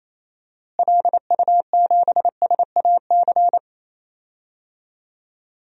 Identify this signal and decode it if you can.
Morse code, heard as a keyed tone, about 28 wpm, 700 Hz. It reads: LU7SAC
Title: DXCC-100 Call Signs - 6 Characters - Repeated 3x - 28wpm